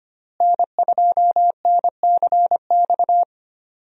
Morse code, heard as a keyed tone, about 25 wpm, 700 Hz. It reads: N2NCX
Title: USA Call Signs - 5 Characters - Repeated 2x - 25wpm